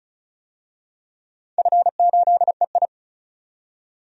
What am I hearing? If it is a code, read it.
Morse code, heard as a keyed tone, about 35 wpm, 700 Hz. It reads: F8EI
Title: DXCC-100 Call Signs - 4 Characters - 35wpm